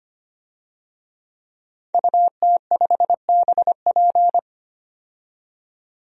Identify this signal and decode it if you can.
Morse code, heard as a keyed tone, about 25 wpm, 700 Hz. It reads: UT5BP
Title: DXCC-100 Call Signs - 5 Characters - 25wpm